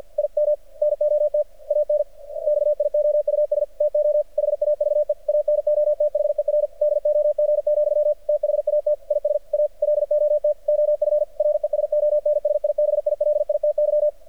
Morse code, heard as a keyed tone, about 48 wpm, 600 Hz. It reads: I M NOT ANXIOUS TO HAVE ANOTHER DOG, THAT IS A LOT OF RESONSIBILITY
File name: archived / CRSnet